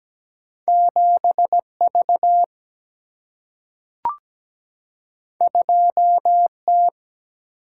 Morse code, heard as a keyed tone, about 17 wpm, 700 Hz. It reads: 7V 2T
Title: Single-Letter Single-Number 17wpm